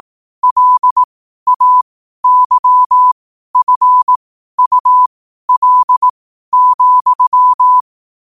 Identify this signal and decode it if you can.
Morse code, heard as a keyed tone, about 18 wpm, spaced as for 12 wpm, 1000 Hz. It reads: LAYFUL,
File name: qotdfiles / 12wpm